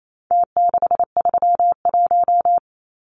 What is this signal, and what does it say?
Morse code, heard as a keyed tone, about 28 wpm, 700 Hz. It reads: T631
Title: Numbers - Contest Style - 4-Digit 28wpm